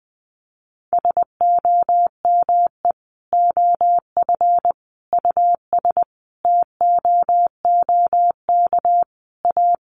Morse code, heard as a keyed tone, about 20 wpm, 700 Hz. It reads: SOME OF US TOOK A
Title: Sentences from Top 1400 Words 20wpm